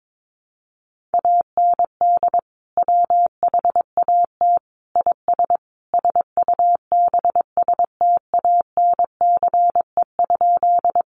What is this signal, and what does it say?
Morse code, heard as a keyed tone, about 22 wpm, 700 Hz. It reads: AND WHAT IS SUBSTANCE?
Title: Sentences from Top 1500 Words 22wpm